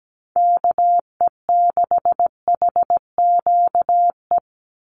Morse code, heard as a keyed tone, about 17 wpm, 700 Hz. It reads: KE6HQE
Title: USA Call Signs 17wpm